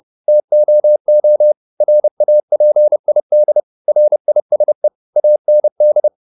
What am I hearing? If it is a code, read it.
Morse code, heard as a keyed tone, about 30 wpm, 600 Hz. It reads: TOO RAPID RISE AND